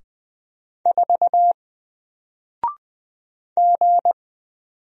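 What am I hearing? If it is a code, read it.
Morse code, heard as a keyed tone, about 20 wpm, 700 Hz. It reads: 4 G
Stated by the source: Single letter and number Morse drill at 20 wpm, from N4LCD.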